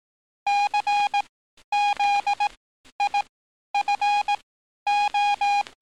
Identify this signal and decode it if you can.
Morse code, heard as a keyed tone, about 18 wpm, spaced as for 11 wpm, 820 Hz. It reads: CZIFO
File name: training / NAVPERS 11049to11060 / NAVPERS 11050 RA RB